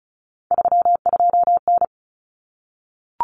Morse code, heard as a keyed tone, about 34 wpm, 700 Hz. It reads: 32N
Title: Common Cut Numbers - 3-Digit 35wpm